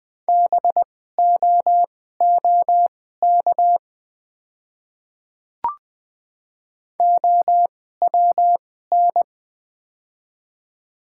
Morse code, heard as a keyed tone, about 20 wpm, spaced as for 14 wpm, 700 Hz. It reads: BOOK OWN
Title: 2x Character Spacing - Top 300 Words - 20wpm